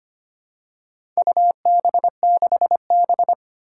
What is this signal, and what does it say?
Morse code, heard as a keyed tone, about 25 wpm, 700 Hz. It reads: UB6B
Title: Four Challenging Letter-Number Combinations 25wpm